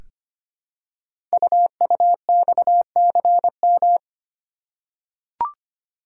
Morse code, heard as a keyed tone, about 25 wpm, 700 Hz. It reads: UUXCM
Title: Code Groups - 5 Letters - 25wpm